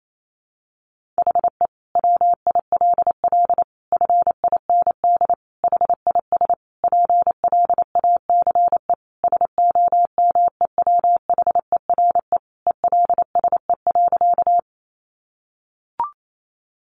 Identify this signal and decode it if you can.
Morse code, heard as a keyed tone, about 28 wpm, 700 Hz. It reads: HE WILL FIND HIS PLACE SOMEWHERE ELSE.
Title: Sentences from Top 1400 Words 28wpm